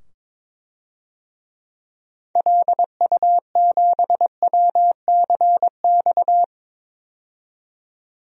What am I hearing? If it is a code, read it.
Morse code, heard as a keyed tone, about 21 wpm, 700 Hz. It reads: LU7WCX
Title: DXCC-100 Call Signs - 6 Characters - 22wpm